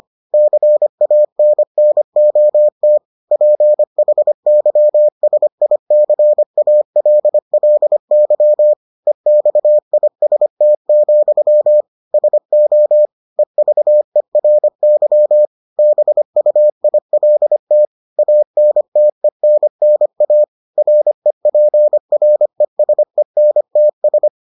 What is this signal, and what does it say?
Morse code, heard as a keyed tone, about 25 wpm, 600 Hz. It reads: CANNOT PHYSICALLY EXIST, SO EVERY BUILT ANTENNA REPRESENTS